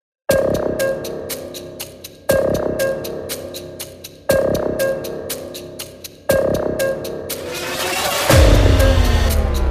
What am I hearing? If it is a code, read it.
There is no code.